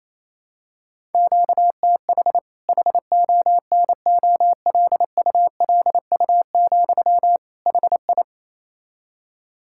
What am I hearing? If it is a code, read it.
Morse code, heard as a keyed tone, about 28 wpm, 700 Hz. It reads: QTH HONOLULU, HI
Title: K1USN SST QSO Practice 28wpm